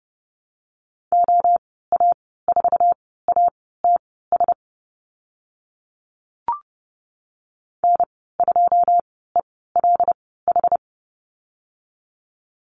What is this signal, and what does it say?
Morse code, heard as a keyed tone, about 30 wpm, spaced as for 17 wpm, 700 Hz. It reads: OA4ATS N2ELH
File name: Lesson 14g - Number 2 - Call Signs 30wpm